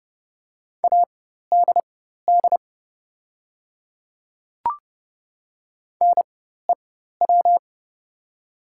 Morse code, heard as a keyed tone, about 30 wpm, spaced as for 14 wpm, 700 Hz. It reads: ADD NEW
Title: 4x Character Spacing - Top 300 Words - 30wpm